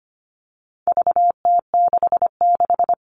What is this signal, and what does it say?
Morse code, heard as a keyed tone, about 25 wpm, 700 Hz. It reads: VT66
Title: All Cut Numbers - 4-Digit 25wpm